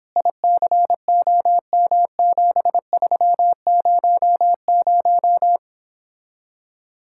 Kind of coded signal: Morse code

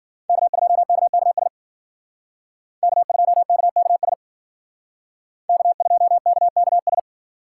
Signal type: Morse code

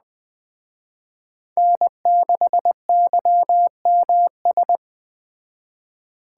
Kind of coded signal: Morse code

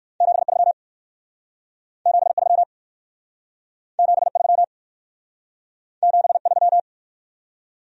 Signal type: Morse code

